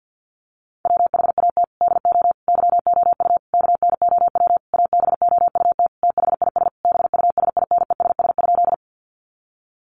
Morse code, heard as a keyed tone, about 48 wpm, 700 Hz. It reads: WHAT DO YOU KNOW ABOUT THIS BUSINESS?
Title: Sentences from Top 600 Words 50wpm